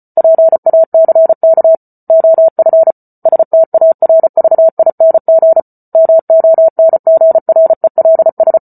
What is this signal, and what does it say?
Morse code, heard as a keyed tone, about 34 wpm, 650 Hz. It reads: PACK OF STARVING MONGRELS